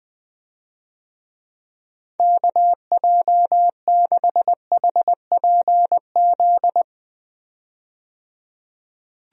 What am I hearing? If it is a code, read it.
Morse code, heard as a keyed tone, about 21 wpm, 700 Hz. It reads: KJ6HPZ
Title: USA Call Signs - 6 Characters - 20wpm